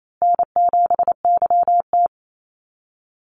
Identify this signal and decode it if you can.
Morse code, heard as a keyed tone, about 28 wpm, 700 Hz. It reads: N7YT
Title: Call Signs Repeated 2x - 28wpm